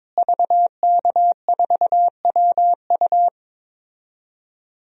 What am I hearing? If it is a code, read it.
Morse code, heard as a keyed tone, about 22 wpm, 700 Hz. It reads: VK4WU
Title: DXCC-100 Call Signs - 5 Characters - Repeated 3x - 22wpm